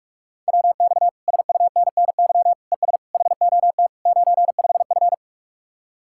Morse code, heard as a keyed tone, about 49 wpm, 700 Hz. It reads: WX SUNNY ES HOT 95F